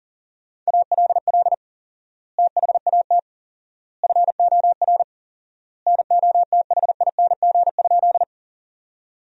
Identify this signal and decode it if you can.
Morse code, heard as a keyed tone, about 40 wpm, 700 Hz. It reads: ALL THAT FOR NOTHING?
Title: 4x Word-Spacing - Sentences from Top 500 Words - 40wpm